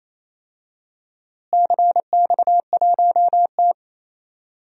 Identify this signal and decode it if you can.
Morse code, heard as a keyed tone, about 28 wpm, 700 Hz. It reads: CX1T
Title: DXCC-100 Call Signs - 4 Characters - Repeated 2x - 28wpm